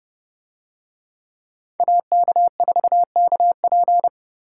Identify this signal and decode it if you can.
Morse code, heard as a keyed tone, about 28 wpm, 700 Hz. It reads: AK4KP